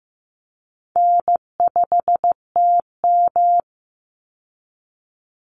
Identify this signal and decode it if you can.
Morse code, heard as a keyed tone, about 15 wpm, 700 Hz. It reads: N5TM
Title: Call Signs Repeated 3x - 15wpm